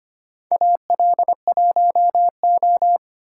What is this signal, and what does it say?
Morse code, heard as a keyed tone, about 25 wpm, 700 Hz. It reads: AL1O
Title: USA Call Signs - 4 Characters - 3x Character Spacing - 25wpm